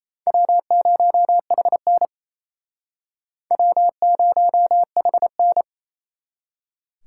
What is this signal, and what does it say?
Morse code, heard as a keyed tone, about 29 wpm, 700 Hz. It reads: W0HN W0HN
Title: Speed Racing - USA 1x2 Call Signs 28wpm